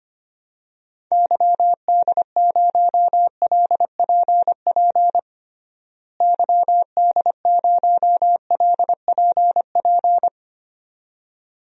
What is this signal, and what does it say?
Morse code, heard as a keyed tone, about 25 wpm, 700 Hz. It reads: YD0LPP YD0LPP
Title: DXCC-100 Call Signs - 6 Characters - Repeated 2x - 25wpm